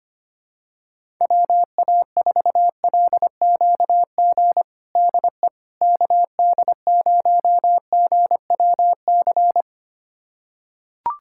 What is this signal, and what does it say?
Morse code, heard as a keyed tone, about 25 wpm, 700 Hz. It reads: WA4LQG DE KD0GWC
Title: USA Callsign DE Callsign - 6 Characters - 25wpm